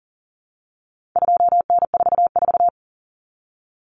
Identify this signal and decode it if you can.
Morse code, heard as a keyed tone, about 40 wpm, 700 Hz. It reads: 2N44
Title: Common Cut Numbers - 4-Digit 40wpm